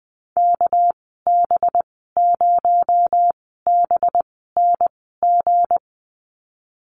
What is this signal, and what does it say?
Morse code, heard as a keyed tone, about 20 wpm, spaced as for 14 wpm, 700 Hz. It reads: KB0BNG
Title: 245 Lesson 39g - Letter z - Call Signs 20wpm